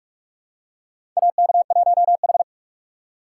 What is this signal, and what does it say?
Morse code, heard as a keyed tone, about 43 wpm, 690 Hz. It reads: AK1H
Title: USA Call Signs - 4x Character Spacing - 45wpm